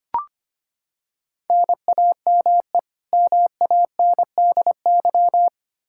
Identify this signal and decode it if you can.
Morse code, heard as a keyed tone, about 25 wpm, 700 Hz. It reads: NAME MANDY
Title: Practice QSO - Exchange 1 - Beginning - 25wpm